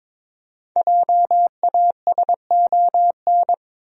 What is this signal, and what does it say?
Morse code, heard as a keyed tone, about 22 wpm, 700 Hz. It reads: JASON
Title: USA Common Names - 22wpm